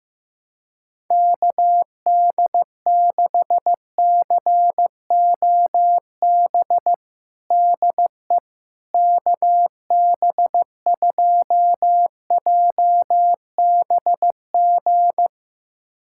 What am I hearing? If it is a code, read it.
Morse code, heard as a keyed tone, about 15 wpm, 700 Hz. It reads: KD6COB DE KB2JBG
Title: USA Callsign DE Callsign - 6 Characters - 15wpm